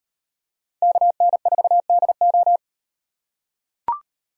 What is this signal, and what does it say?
Morse code, heard as a keyed tone, about 38 wpm, 700 Hz. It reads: KN4DO